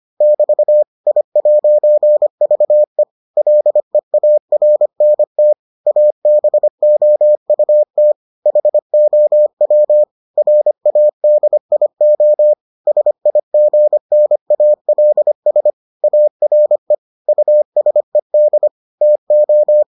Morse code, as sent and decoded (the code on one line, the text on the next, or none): -...- / .. .----. ...- . / .-.. .- .-. -. - / .- -... --- ..- - / .... --- .-- / .-. .- -.. .. --- / ... .. --. -. .- .-.. ... / .- .-. / ..- ... -.. / - ---
<BT> I'VE LEARNT ABOUT HOW RADIO SIGNALS ARE USED TO